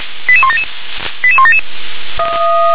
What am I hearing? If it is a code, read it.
Five-tone selective call (ZVEI), heard as quick runs of five tones, tones of about 70 ms. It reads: 88177 88177